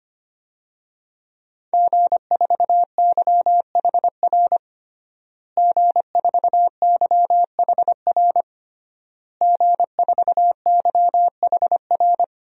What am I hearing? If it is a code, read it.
Morse code, heard as a keyed tone, about 25 wpm, 700 Hz. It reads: G4YHR G4YHR G4YHR
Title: DXCC-100 Call Signs - 5 Characters - Repeated 3x - 25wpm